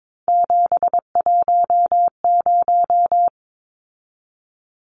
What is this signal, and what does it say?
Morse code, heard as a keyed tone, about 22 wpm, 700 Hz. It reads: 710
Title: Numbers - 3-Digit 22wpm